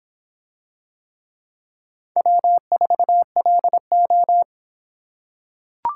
Morse code, heard as a keyed tone, about 22 wpm, 700 Hz. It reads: W4LO